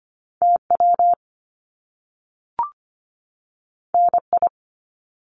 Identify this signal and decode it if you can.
Morse code, heard as a keyed tone, about 25 wpm, 700 Hz. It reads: TW NI
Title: Top Two Letter Combinations in Words 25wpm